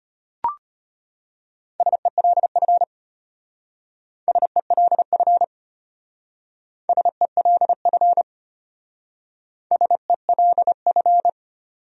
Morse code, heard as a keyed tone, about 31 wpm, 700 Hz. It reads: SELF SELF SELF SELF